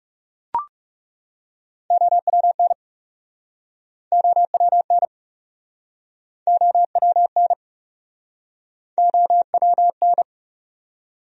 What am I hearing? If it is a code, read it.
Morse code, heard as a keyed tone, about 40 wpm, 700 Hz. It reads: OWN OWN OWN OWN